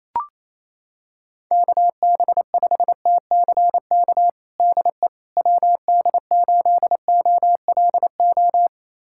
Morse code, heard as a keyed tone, about 28 wpm, 700 Hz. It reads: KB5TCK DE WD8OLO
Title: USA Callsign DE Callsign - 6 Characters - 28wpm